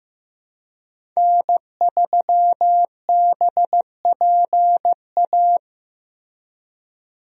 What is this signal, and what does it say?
Morse code, heard as a keyed tone, about 15 wpm, 700 Hz. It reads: N3BPA
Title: USA Call Signs - 5 Characters - Repeated 2x - 15wpm